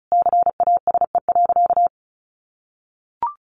Morse code, heard as a keyed tone, about 34 wpm, 700 Hz. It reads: CASE.
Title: Sentences from Top 1000 Words 35wpm